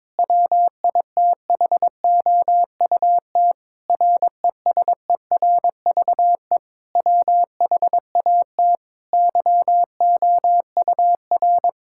Morse code, heard as a keyed tone, about 22 wpm, 700 Hz. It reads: WITHOUT RESERVE WHAT YOUR
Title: Sentences from Top 1000 Words - Review Part 6 - 22wpm